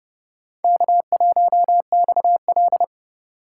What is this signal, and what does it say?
Morse code, heard as a keyed tone, about 30 wpm, 700 Hz. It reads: K1XL
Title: USA Call Signs - 4 Characters - 4x Character Spacing - 30wpm